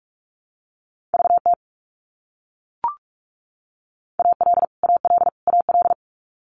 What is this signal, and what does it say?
Morse code, heard as a keyed tone, about 44 wpm, 700 Hz. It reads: VT AL AL AL
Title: US States Repeated 3x 45wpm